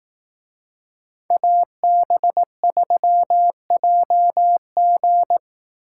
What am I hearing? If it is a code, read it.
Morse code, heard as a keyed tone, about 18 wpm, 700 Hz. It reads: AB3JG
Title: Speed Racing - USA 2x2 Call Signs 15wpm